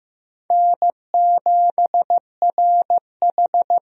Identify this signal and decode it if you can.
Morse code, heard as a keyed tone, about 16 wpm, 700 Hz. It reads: N7RH
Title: USA Call Signs - 4 Characters - Repeated 3x - 15wpm